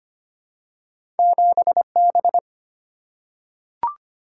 Morse code, heard as a keyed tone, about 25 wpm, 700 Hz. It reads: 7 B
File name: Two Letter-Number Combinations - 25wpm